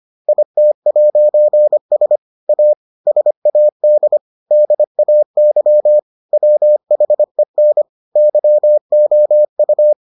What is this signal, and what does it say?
Morse code, heard as a keyed tone, about 25 wpm, 600 Hz. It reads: IT'S A SAD DAY WHEN YOU